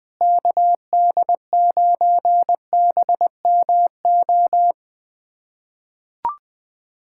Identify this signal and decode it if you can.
Morse code, heard as a keyed tone, about 21 wpm, 700 Hz. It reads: KD9BMO
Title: USA Callsign DE Callsign - 6 Characters - 20wpm